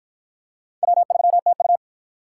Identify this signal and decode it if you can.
Morse code, heard as a keyed tone, about 52 wpm, 700 Hz. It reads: W3TU